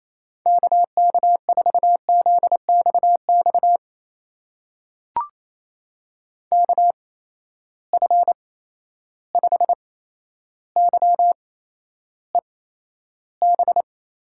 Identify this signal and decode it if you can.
Morse code, heard as a keyed tone, about 28 wpm, spaced as for 12 wpm, 700 Hz. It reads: KK4ZXX KF5YEB
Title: USA Call Signs - 6 Characters - 8x Character Spacing - 28wpm